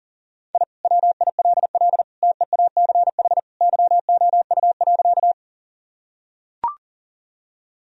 Morse code, heard as a keyed tone, about 37 wpm, 700 Hz. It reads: I WILL TEACH YOU.
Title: Sentences from Top 1000 Words - Review Part 2 - 40wpm